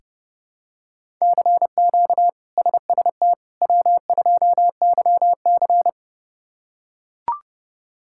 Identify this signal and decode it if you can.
Morse code, heard as a keyed tone, about 30 wpm, 700 Hz. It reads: CQ SST W2YC
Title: K1USN SST QSO Practice - 30wpm